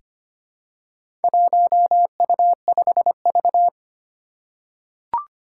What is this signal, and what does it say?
Morse code, heard as a keyed tone, about 25 wpm, 700 Hz. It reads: 1U5V
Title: Four Challenging Letter-Number Combinations 25wpm